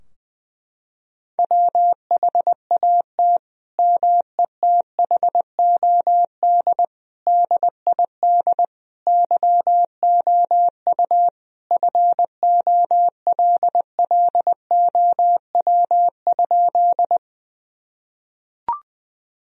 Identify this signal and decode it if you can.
Morse code, heard as a keyed tone, about 20 wpm, 700 Hz. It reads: WHAT METHOD DID YOU FOLLOW?
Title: Sentences from Top 1000 Words - Review Part 3 - 20wpm